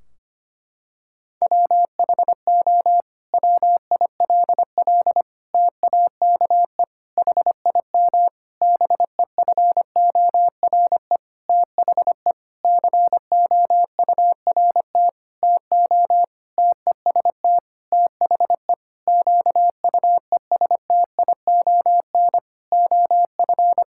Morse code, heard as a keyed tone, about 25 wpm, 700 Hz. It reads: WHO WILL TAKE HIM BEFORE THE COURT TO TEST THE QUESTION OF
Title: Sentences from Top 1600 Words 25wpm